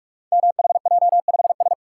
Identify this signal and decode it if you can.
Morse code, heard as a keyed tone, about 44 wpm, 700 Hz. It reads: MHJ5S
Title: Code Groups - 5 Letter:Numbers - 45wpm